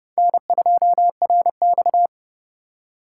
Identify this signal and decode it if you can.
Morse code, heard as a keyed tone, about 26 wpm, 700 Hz. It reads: N2RX